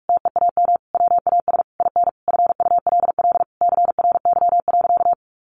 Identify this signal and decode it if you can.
Morse code, heard as a keyed tone, about 44 wpm, 700 Hz. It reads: TEAM WAS IN FULL CRY.
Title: Sentences from Top 1500 Words 45wpm